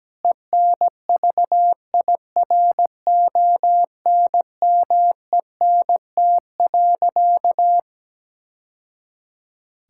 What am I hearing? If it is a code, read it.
Morse code, heard as a keyed tone, about 17 wpm, 700 Hz. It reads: ENVIRONMENT.
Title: Sentences from Top 1000 Words - Review Part 5 - 17wpm